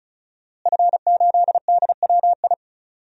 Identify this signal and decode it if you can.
Morse code, heard as a keyed tone, about 36 wpm, 700 Hz. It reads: F8DWI
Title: DXCC-100 Call Signs - 5 Characters - Repeated 3x - 35wpm